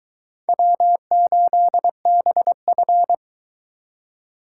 Morse code, heard as a keyed tone, about 24 wpm, 700 Hz. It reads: W8BF